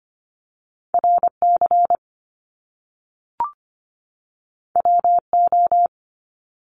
Morse code, heard as a keyed tone, about 25 wpm, 700 Hz. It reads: RC WO
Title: Top Two Letter Combinations in Words 25wpm